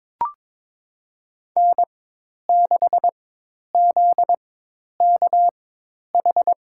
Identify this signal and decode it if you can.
Morse code, heard as a keyed tone, about 22 wpm, spaced as for 10 wpm, 700 Hz. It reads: N6ZKH
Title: USA Call Signs - 5 Characters - 4x Character Spacing - 22wpm